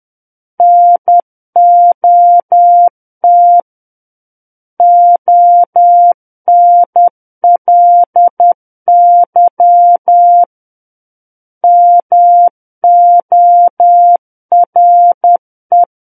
Morse code, heard as a keyed tone, about 10 wpm, 700 Hz. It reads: NOT ONLY MORE
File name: Book One / 01-Chapter 1 - 10 WPM